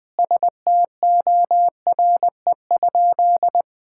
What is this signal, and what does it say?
Morse code, heard as a keyed tone, about 19 wpm, 700 Hz. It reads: STORE?
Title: Sentences from Top 1000 Words - Review Part 5 - 20wpm